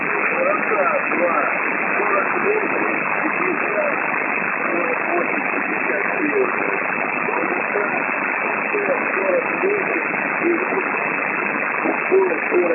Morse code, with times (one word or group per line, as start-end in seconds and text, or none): none